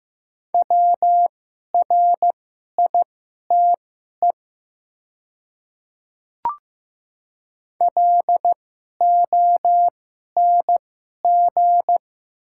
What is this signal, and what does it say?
Morse code, heard as a keyed tone, about 15 wpm, spaced as for 11 wpm, 700 Hz. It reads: WRITE LONG
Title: 2x Character Spacing - Top 100 Words - 15wpm